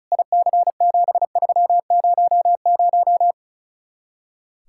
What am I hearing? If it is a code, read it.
Morse code, heard as a keyed tone, about 42 wpm, 700 Hz. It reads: IC7300
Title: Speed-Racing Common QSO Phrases Encore - 35wpm